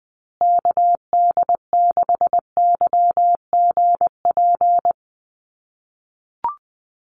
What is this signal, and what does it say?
Morse code, heard as a keyed tone, about 20 wpm, 700 Hz. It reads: KD6YGP
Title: USA Call Signs Repeated 3x 20wpm